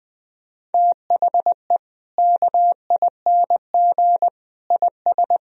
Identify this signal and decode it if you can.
Morse code, heard as a keyed tone, about 21 wpm, 700 Hz. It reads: THE KING IS
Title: Sentences from Top 1000 Words - Review Part 1 - 20wpm